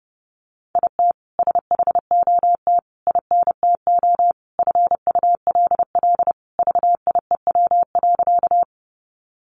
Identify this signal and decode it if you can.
Morse code, heard as a keyed tone, about 30 wpm, 700 Hz. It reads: IT SHOT INTO FULL VIEW.
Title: Sentences from Top 1100 Words 30wpm